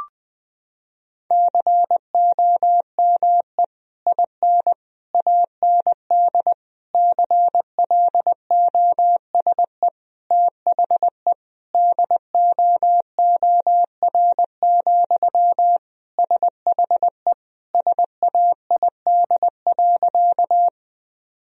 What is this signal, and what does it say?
Morse code, heard as a keyed tone, about 20 wpm, 700 Hz. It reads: COME IN AND CLOSE THE DOOR, SHE SAID.
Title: Sentences from Top 1000 Words - Review Part 1 - 20wpm